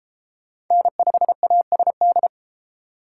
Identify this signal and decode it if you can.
Morse code, heard as a keyed tone, about 30 wpm, 700 Hz. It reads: N5ASD